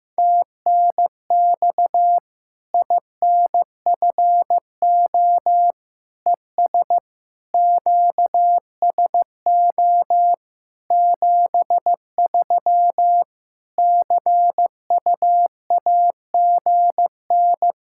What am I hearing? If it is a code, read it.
Morse code, heard as a keyed tone, about 15 wpm, 700 Hz. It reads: TNX INFO ES QSO 73 CUAGN
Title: Practice QSO - Exchange 4 - Ending - 15wpm